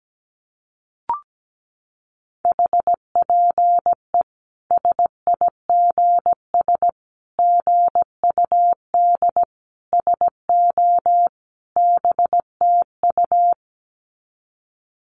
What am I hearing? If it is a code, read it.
Morse code, heard as a keyed tone, about 17 wpm, 700 Hz. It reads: HPE SIGS GUD SO BTU
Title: Common QSO Phrases 17wpm